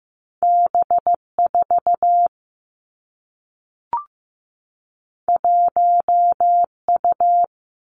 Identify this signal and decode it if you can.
Morse code, heard as a keyed tone, about 15 wpm, 700 Hz. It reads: B 4 1 U
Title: Two Challenging Letter-Number Combinations 15wpm